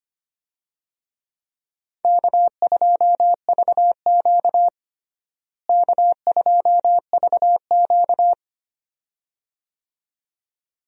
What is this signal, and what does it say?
Morse code, heard as a keyed tone, about 25 wpm, 700 Hz. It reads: K2VQ K2VQ
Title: USA Call Signs - 4 Characters - Repeated 2x - 25wpm